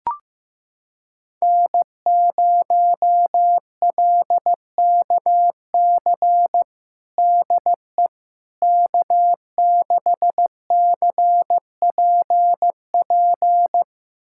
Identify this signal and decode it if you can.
Morse code, heard as a keyed tone, about 16 wpm, 700 Hz. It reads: N0LKC DE K6CPP
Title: USA Callsign DE Callsign - 5 Characters - 15wpm